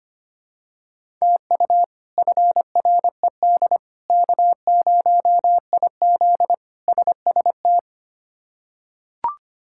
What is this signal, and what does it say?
Morse code, heard as a keyed tone, about 25 wpm, 700 Hz. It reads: TU FRED K0IZ SST
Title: K1USN SST QSO Practice - 25wpm